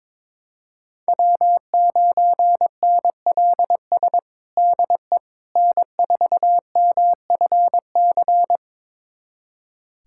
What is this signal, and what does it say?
Morse code, heard as a keyed tone, about 22 wpm, 700 Hz. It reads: W9NLS DE N4MFC
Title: USA Callsign DE Callsign 22wpm